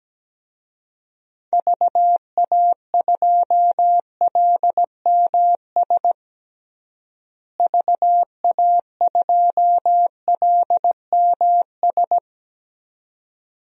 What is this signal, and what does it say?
Morse code, heard as a keyed tone, about 17 wpm, 700 Hz. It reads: VA2LMS VA2LMS
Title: DXCC-100 Call Signs - 6 Characters - Repeated 2x - 17wpm